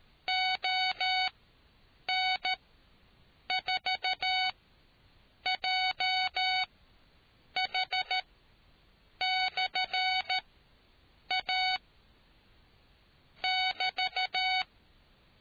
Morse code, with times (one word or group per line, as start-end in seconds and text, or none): 0.3-11.8 ON4JH/A
13.4-14.6 <BT>